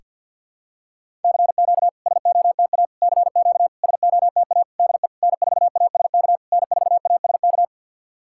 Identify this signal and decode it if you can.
Morse code, heard as a keyed tone, about 52 wpm, 690 Hz. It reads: CQ SOTA CQ SOTA DE N4ASK N4ASK
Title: SOTA QSO Practice 50wpm